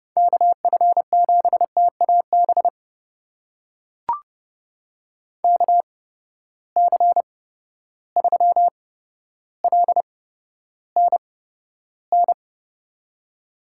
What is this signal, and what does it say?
Morse code, heard as a keyed tone, about 29 wpm, spaced as for 13 wpm, 700 Hz. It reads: KF7TAB KC3LNN
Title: USA Call Signs - 6 Characters - 8x Character Spacing - 30wpm